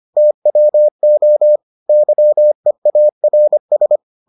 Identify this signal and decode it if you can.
Morse code, heard as a keyed tone, about 25 wpm, 600 Hz. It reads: TWO YEARS